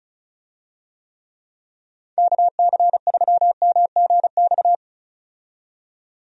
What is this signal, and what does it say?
Morse code, heard as a keyed tone, about 34 wpm, 700 Hz. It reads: KC3MGX
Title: USA Call Signs - 6 Characters - 35wpm